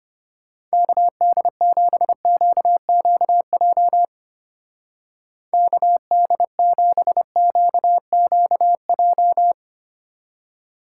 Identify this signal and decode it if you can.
Morse code, heard as a keyed tone, about 31 wpm, 700 Hz. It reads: KD7QQJ KD7QQJ
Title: Speed Racing - USA Call Signs 25wpm